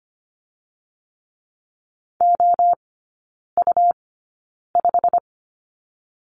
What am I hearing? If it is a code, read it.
Morse code, heard as a keyed tone, about 25 wpm, 700 Hz. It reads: O U 5
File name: Lesson 17e - Letter u - Three Characters 25wpm